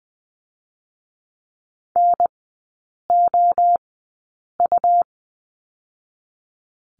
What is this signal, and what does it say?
Morse code, heard as a keyed tone, about 20 wpm, spaced as for 8 wpm, 700 Hz. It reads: NOU
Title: 100 Lesson 17e - Letter u - Three Characters 20wpm